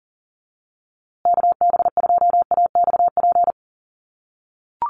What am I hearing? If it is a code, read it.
Morse code, heard as a keyed tone, about 40 wpm, 700 Hz. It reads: KB2AXP
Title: USA Call Signs Repeated 2x 40wpm